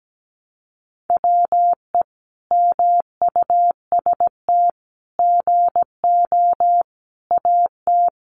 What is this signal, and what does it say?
Morse code, heard as a keyed tone, about 17 wpm, 700 Hz. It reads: WE MUST GO AT
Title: Sentences from Top 300 Words 17wpm